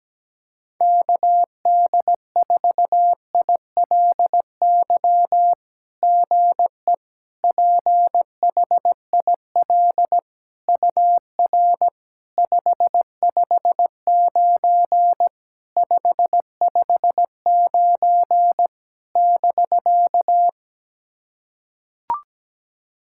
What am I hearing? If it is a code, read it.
Morse code, heard as a keyed tone, about 17 wpm, 700 Hz. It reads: KD4ILY GE PHIL UR 559 559 <BK>
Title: SOTA QSO Practice 17wpm